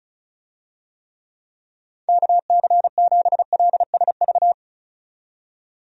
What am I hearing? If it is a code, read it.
Morse code, heard as a keyed tone, about 36 wpm, 700 Hz. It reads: KC7LSV
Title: USA Call Signs - 6 Characters - 35wpm